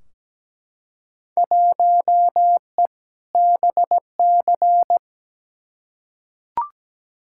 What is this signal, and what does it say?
Morse code, heard as a keyed tone, about 17 wpm, 700 Hz. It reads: 1E BC
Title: Field Day Exchanges - 17wpm